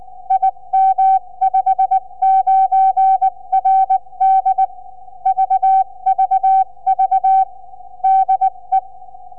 Morse code, heard as a keyed tone, about 19 wpm, 770 Hz. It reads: IM59RD VVV DE
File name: cs5bfm 4m